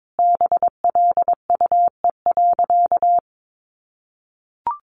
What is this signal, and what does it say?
Morse code, heard as a keyed tone, about 22 wpm, 700 Hz. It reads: BLUE.
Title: Additional Easy Phrases 22wpm